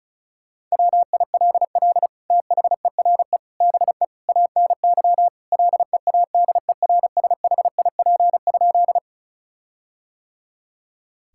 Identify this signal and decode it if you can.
Morse code, heard as a keyed tone, about 36 wpm, 700 Hz. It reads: WILL THERE BE ANY LEADERSHIP?
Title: Sentences from Top 1600 Words 35wpm